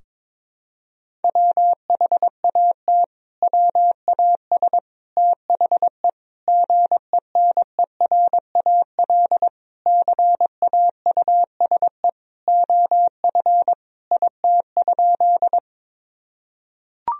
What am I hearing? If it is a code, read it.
Morse code, heard as a keyed tone, about 22 wpm, 700 Hz. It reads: WHAT WAS THE GENERAL CAUSE OF IT?
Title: Sentences from Top 1000 Words - Review Part 3 - 22wpm